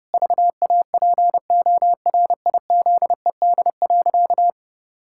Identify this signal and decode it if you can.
Morse code, heard as a keyed tone, about 30 wpm, 700 Hz. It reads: VAPORIZED.
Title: Pangram Challenge v2 - 30wpm